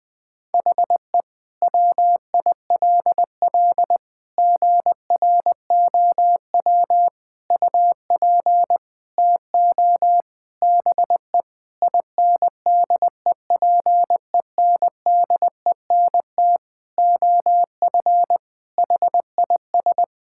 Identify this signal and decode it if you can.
Morse code, heard as a keyed tone, about 21 wpm, 700 Hz. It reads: HE WILL GROW UP TO BE INDEPENDENT OF HIS
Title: Sentences from Top 1000 Words - Review Part 6 - 20wpm